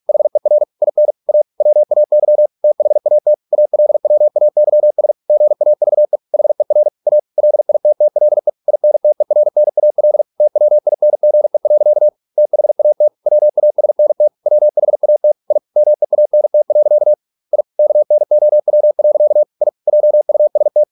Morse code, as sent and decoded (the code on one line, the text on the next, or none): .... .-. / .. -. / .- / .-- .- -.-- / - .... .- - / .- .-.. .-- .- -.-- ... / --. .- ...- . / .... .-. / .- / .-.. .. - - .-.. / .. -. - . .-. -. .- .-.. / - .-- .. -. --. .-.-.- / - .... .- - / .-- .- ... -. - / .-- .... .- - / .. / -- . .- -. - .-.-.- / .. / -.- -. --- .-- .-.-.- / .. / .--- ..- ... -
HER IN A WAY THAT ALWAYS GAVE HER A LITTLE INTERNAL TWINGE. THAT WASNT WHAT I MEANT. I KNOW. I JUST